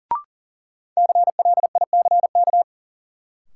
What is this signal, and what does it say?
Morse code, heard as a keyed tone, about 40 wpm, 700 Hz. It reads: CLICK
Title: Sentences from Top 1100 Words - New Word Drill - 40wpm